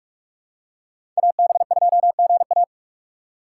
Morse code, heard as a keyed tone, about 41 wpm, 700 Hz. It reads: AB2GA
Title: USA Call Signs - 5 Characters - 2x Character Spacing - 45wpm